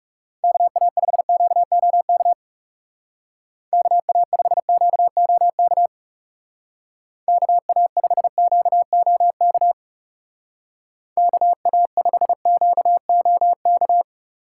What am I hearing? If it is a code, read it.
Morse code, heard as a keyed tone, about 36 wpm, 700 Hz. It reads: KA5QOK KA5QOK KA5QOK KA5QOK
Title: Speed Racing - USA 2x3 Call Signs 30wpm